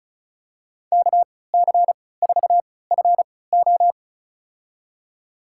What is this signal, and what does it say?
Morse code, heard as a keyed tone, about 36 wpm, spaced as for 20 wpm, 700 Hz. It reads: KC4FO
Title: USA Call Signs - 5 Characters - 3x Character Spacing - 35wpm